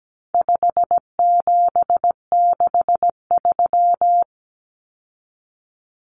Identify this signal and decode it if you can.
Morse code, heard as a keyed tone, about 17 wpm, 700 Hz. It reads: 5763
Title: Numbers - 4-Digit 17wpm